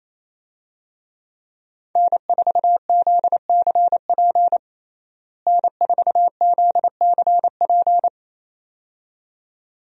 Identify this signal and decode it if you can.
Morse code, heard as a keyed tone, about 27 wpm, 700 Hz. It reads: N4ZCP N4ZCP
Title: USA Call Signs - 5 Characters - Repeated 2x - 28wpm